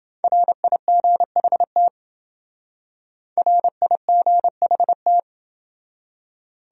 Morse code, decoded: RIGHT RIGHT

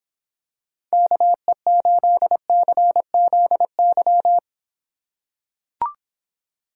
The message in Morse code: KE8CZY

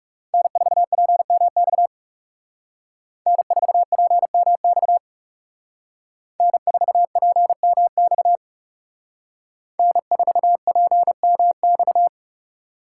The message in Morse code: N4PMX N4PMX N4PMX N4PMX

